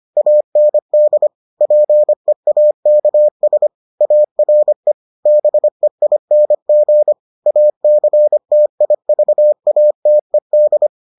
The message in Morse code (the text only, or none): AND PEAKS ARE BEING ACTIVATED